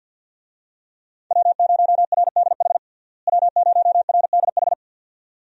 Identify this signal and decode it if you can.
Morse code, heard as a keyed tone, about 51 wpm, 700 Hz. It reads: W0RDH W0RDH
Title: USA Call Signs - 5 Characters - Repeated 2x - 50wpm